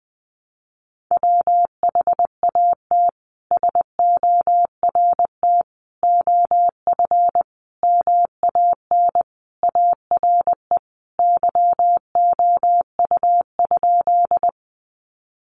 Morse code, heard as a keyed tone, about 20 wpm, 700 Hz. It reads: WHAT SORT OF MAN ARE YOU?
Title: Sentences from Top 1300 Words 20wpm